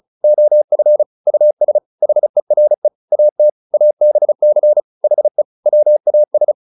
Morse code, heard as a keyed tone, about 34 wpm, 600 Hz. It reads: OF US HERE AT ABC HE WAS